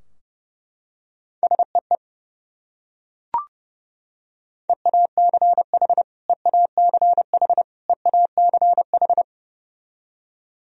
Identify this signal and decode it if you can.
Morse code, heard as a keyed tone, about 32 wpm, 700 Hz. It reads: SEE EACH EACH EACH
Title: Top 100 Words Repeated 30wpm